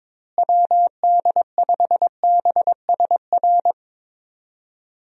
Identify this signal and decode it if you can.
Morse code, heard as a keyed tone, about 22 wpm, 700 Hz. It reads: WD5BSR
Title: USA Call Signs - 6 Characters - Repeated 3x - 22wpm